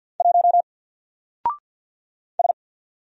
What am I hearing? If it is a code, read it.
Morse code, heard as a keyed tone, about 48 wpm, 700 Hz. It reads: 1 S
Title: Single Character (Letter:Number:Punctuation:Prosign) Alt 50wpm